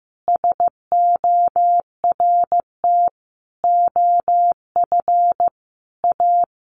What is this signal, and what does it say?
Morse code, heard as a keyed tone, about 15 wpm, 700 Hz. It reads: SORT OF A
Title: Sentences from Top 1300 Words 15wpm